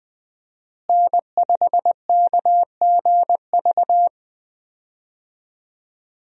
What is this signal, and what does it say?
Morse code, heard as a keyed tone, about 21 wpm, 700 Hz. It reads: N5KGV
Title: USA Call Signs - 5 Characters - Repeated 2x - 20wpm